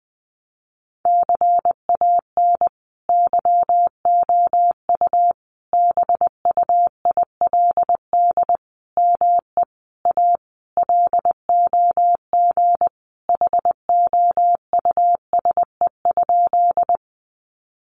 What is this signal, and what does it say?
Morse code, heard as a keyed tone, about 20 wpm, 700 Hz. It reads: CAN YOU BUILD ME A LOG HOUSE?
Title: Sentences from Top 1400 Words 20wpm